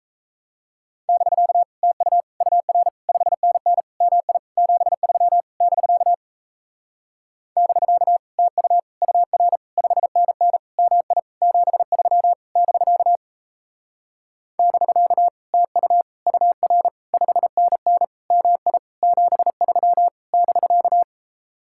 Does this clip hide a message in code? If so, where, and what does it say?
Yes: Morse code from 1.1 to 21.0 s, heard as a keyed tone, about 38 wpm, 700 Hz. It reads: <BK> TU UR 5NN MI 73 <BK> <BK> TU UR 5NN MI 73 <BK> <BK> TU UR 5NN MI 73 <BK>